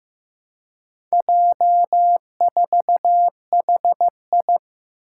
Morse code, heard as a keyed tone, about 15 wpm, 700 Hz. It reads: J4HI
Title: Four Challenging Letter-Number Combinations 15wpm